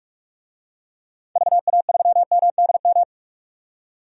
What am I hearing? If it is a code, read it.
Morse code, heard as a keyed tone, about 46 wpm, 690 Hz. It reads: VA3MDM
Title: DXCC-100 Call Signs - 6 Characters - 45wpm